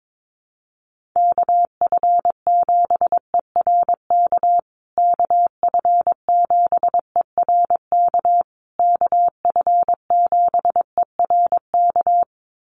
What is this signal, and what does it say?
Morse code, heard as a keyed tone, about 22 wpm, 700 Hz. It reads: KF7ERK KF7ERK KF7ERK
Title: USA Call Signs Repeated 3x 22wpm